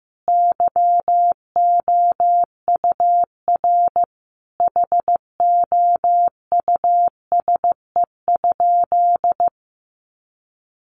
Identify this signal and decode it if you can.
Morse code, heard as a keyed tone, about 15 wpm, 700 Hz. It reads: YOUR HOUSE?
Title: Sentences from Top 1500 Words 15wpm